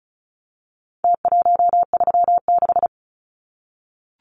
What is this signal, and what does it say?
Morse code, heard as a keyed tone, about 34 wpm, 700 Hz. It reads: T136
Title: Numbers - Contest Style - 4-Digit 35wpm